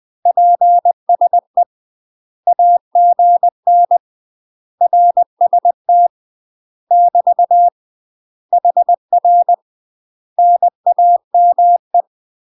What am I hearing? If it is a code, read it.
Morse code, heard as a keyed tone, about 20 wpm, 700 Hz. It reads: PSE AGN RST <BT> HR NAME
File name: UK MASTER QSOs TRAINING / 20WPM / UK MASTER QSOs 0014